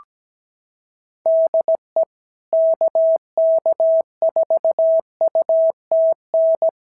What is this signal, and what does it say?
Morse code, heard as a keyed tone, about 17 wpm, 650 Hz. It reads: DE KK4UTN
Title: SOTA QSO Practice 17wpm